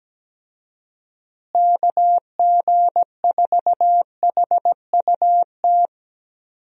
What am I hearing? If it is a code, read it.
Morse code, heard as a keyed tone, about 17 wpm, 700 Hz. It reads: KG4HUT
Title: USA Call Signs - 6 Characters - Repeated 3x - 17wpm